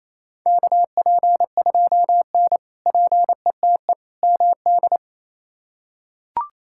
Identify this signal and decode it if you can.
Morse code, heard as a keyed tone, about 28 wpm, 700 Hz. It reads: KP2N PETE MD
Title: K1USN SST QSO Practice - 28wpm